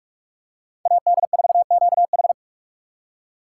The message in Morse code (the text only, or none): AD4QH